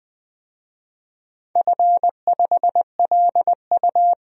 Morse code, heard as a keyed tone, about 20 wpm, 700 Hz. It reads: F5LU
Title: DXCC-100 Call Signs - 4 Characters - 20wpm